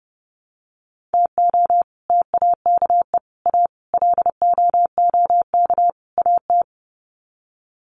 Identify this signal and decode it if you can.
Morse code, heard as a keyed tone, about 30 wpm, 700 Hz. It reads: TO TAKE A LOOK AT
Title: Sets of 5 Words 30wpm